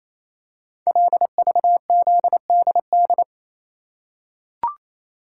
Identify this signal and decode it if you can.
Morse code, heard as a keyed tone, about 27 wpm, 700 Hz. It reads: LVZDD
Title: Code Groups - 5 Letters - 28wpm